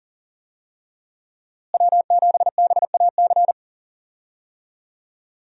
Morse code, heard as a keyed tone, about 40 wpm, 700 Hz. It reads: W7BAC
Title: USA Call Signs - 5 Characters - Repeated 3x - 40wpm